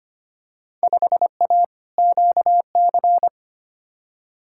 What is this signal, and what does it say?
Morse code, heard as a keyed tone, about 25 wpm, 700 Hz. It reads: 5A QC
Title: Field Day Exchanges Encore - 25wpm